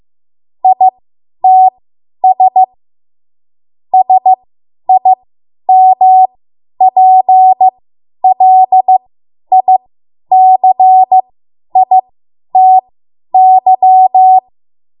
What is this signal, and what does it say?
Morse code, heard as a keyed tone, about 15 wpm, spaced as for 10 wpm, 750 Hz. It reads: ITS SIMPLICITY
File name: Morse / 10WPM